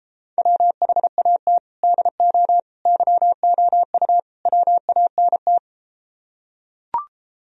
Morse code, heard as a keyed tone, about 31 wpm, 700 Hz. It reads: WHAT DO YOU WANT